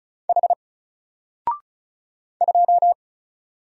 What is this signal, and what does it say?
Morse code, heard as a keyed tone, about 36 wpm, 700 Hz. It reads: H 2
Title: Single Character (Letter:Number:Punctuation:Prosign) Alt 35wpm